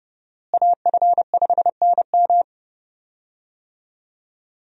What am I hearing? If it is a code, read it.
Morse code, heard as a keyed tone, about 30 wpm, 700 Hz. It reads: AF5NM
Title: USA Call Signs - 5 Characters - Repeated 2x - 30wpm